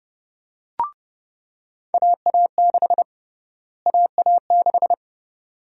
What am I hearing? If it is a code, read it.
Morse code, heard as a keyed tone, about 30 wpm, 700 Hz. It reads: AA6 AA6
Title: USA Call Sign Prefixes - Repeated 2x - 30wpm